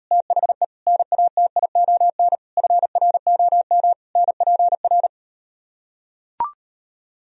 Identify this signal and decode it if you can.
Morse code, heard as a keyed tone, about 38 wpm, 700 Hz. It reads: THE NATION FROM NPR